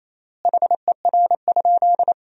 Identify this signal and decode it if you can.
Morse code, heard as a keyed tone, about 27 wpm, 700 Hz. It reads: HER?
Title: 3x Word-Spacing - Sentences from Top 100 Words - 28wpm